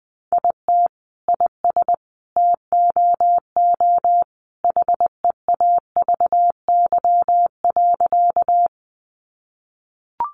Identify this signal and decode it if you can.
Morse code, heard as a keyed tone, about 20 wpm, 700 Hz. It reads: IT IS TOO HEAVY.
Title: Sentences from Top 1100 Words 20wpm